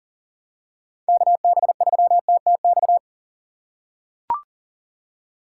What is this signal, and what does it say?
Morse code, heard as a keyed tone, about 40 wpm, 700 Hz. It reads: KB3TTX
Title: USA Call Signs - 6 Characters - 6x Character Spacing - 40wpm